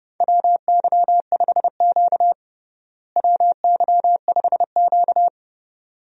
Morse code, heard as a keyed tone, about 29 wpm, 700 Hz. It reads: WY5Q WY5Q
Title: USA Call Signs - 4 Characters - Repeated 2x - 30wpm